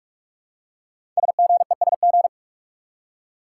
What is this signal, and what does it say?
Morse code, heard as a keyed tone, about 44 wpm, 700 Hz. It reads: SGESG
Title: Code Groups - 5 Letters - 45wpm